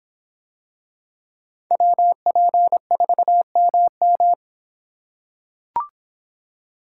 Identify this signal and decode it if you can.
Morse code, heard as a keyed tone, about 25 wpm, 700 Hz. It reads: WP4MM